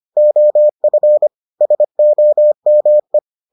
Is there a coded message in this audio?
Yes: Morse code from 0.2 to 3.2 s, heard as a keyed tone, about 25 wpm, 600 Hz. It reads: OF SOME